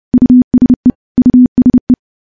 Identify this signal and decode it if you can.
Morse code, heard as a keyed tone, about 31 wpm, 250 Hz. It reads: USE USE
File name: morse / morse-training-30WPM